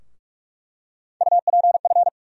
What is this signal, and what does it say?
Morse code, heard as a keyed tone, about 42 wpm, 700 Hz. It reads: UPF
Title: USA Call Sign Postfixes - 3 Letters 45wpm